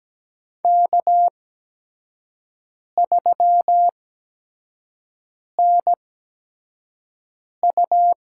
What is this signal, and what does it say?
Morse code, heard as a keyed tone, about 17 wpm, spaced as for 5 wpm, 700 Hz. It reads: K3NU
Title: USA Call Signs - 4 Characters - 8x Character Spacing - 17wpm